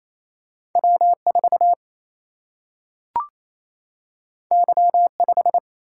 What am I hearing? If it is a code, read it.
Morse code, heard as a keyed tone, about 28 wpm, 700 Hz. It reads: W4 Y5
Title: Single-Letter Single-Number 28wpm